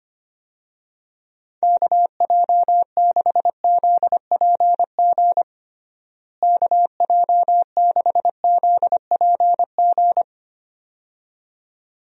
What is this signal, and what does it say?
Morse code, heard as a keyed tone, about 25 wpm, 700 Hz. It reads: KJ6ZPG KJ6ZPG
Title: USA Call Signs - 6 Characters - Repeated 2x - 25wpm